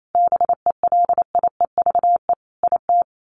Morse code, heard as a keyed tone, about 28 wpm, 700 Hz. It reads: BELIEVE IT
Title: Sets of 5 Words Encore 28wpm